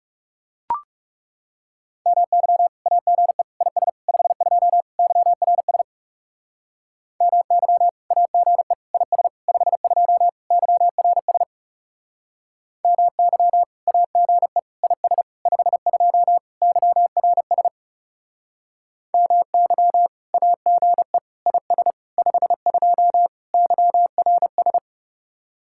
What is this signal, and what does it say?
Morse code, heard as a keyed tone, about 37 wpm, 700 Hz. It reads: MY AGE IS 52 YRS MY AGE IS 52 YRS MY AGE IS 52 YRS MY AGE IS 52 YRS
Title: Speed-Racing Common QSO Phrases - 30wpm